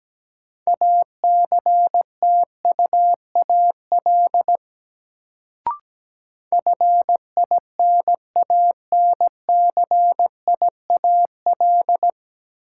Morse code, heard as a keyed tone, about 17 wpm, 700 Hz. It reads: ACTUAL FINANCIAL
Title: Sentences from Top 1000 Words - Review Part 5 - New Word Drill - 17wpm